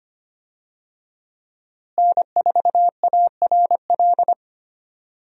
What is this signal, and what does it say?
Morse code, heard as a keyed tone, about 25 wpm, 700 Hz. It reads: N4ARL
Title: USA Call Signs - 5 Characters - Repeated 2x - 25wpm